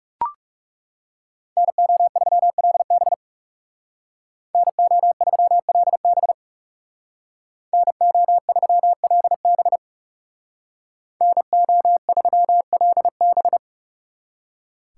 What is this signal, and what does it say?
Morse code, heard as a keyed tone, about 37 wpm, 700 Hz. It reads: NO3LB NO3LB NO3LB NO3LB
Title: Speed Racing - USA Call Signs 30wpm